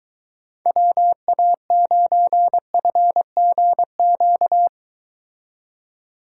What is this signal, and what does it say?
Morse code, heard as a keyed tone, about 23 wpm, 700 Hz. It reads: WA9FGQ